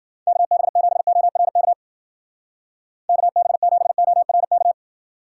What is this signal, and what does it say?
Morse code, heard as a keyed tone, about 49 wpm, 700 Hz. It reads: KB7ORK KB7ORK
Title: Speed Racing - USA 2x3 Call Signs 40wpm